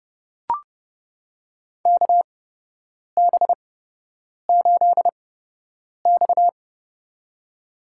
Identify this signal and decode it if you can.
Morse code, heard as a keyed tone, about 31 wpm, spaced as for 8 wpm, 700 Hz. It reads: KB8X
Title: USA Call Signs - 4 Characters - 8x Character Spacing - 30wpm